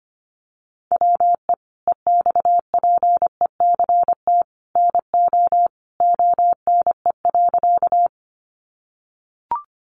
Morse code, heard as a keyed tone, about 25 wpm, 700 Hz. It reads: WE EXPECT NO ONE.
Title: Sentences from Top 1100 Words 25wpm